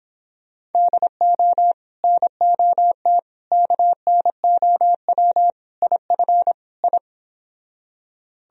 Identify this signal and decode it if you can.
Morse code, heard as a keyed tone, about 29 wpm, 700 Hz. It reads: DO NOT KNOW IF I